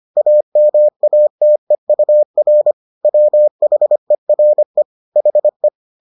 Morse code, heard as a keyed tone, about 25 wpm, 600 Hz. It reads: AMATEUR WHERE HE